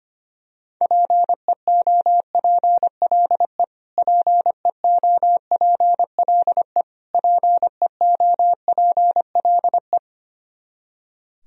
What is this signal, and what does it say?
Morse code, heard as a keyed tone, about 25 wpm, 700 Hz. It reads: PEOPLE PEOPLE PEOPLE
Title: Top 100 Words Repeated 25wpm